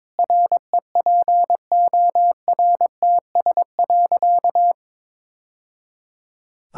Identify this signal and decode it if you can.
Morse code, heard as a keyed tone, about 21 wpm, 700 Hz. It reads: REPORTS.
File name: practice / 22wpm / Easy Sentences with Hints - 22wpm